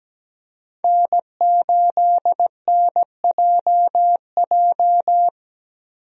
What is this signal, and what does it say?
Morse code, heard as a keyed tone, about 17 wpm, 700 Hz. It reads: N8NJJ
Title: USA Call Signs - 5 Characters - Repeated 2x - 17wpm